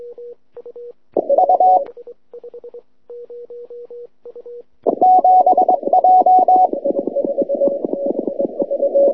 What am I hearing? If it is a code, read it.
Morse code, heard as a keyed tone, about 22 wpm, 740 Hz. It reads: U 7 J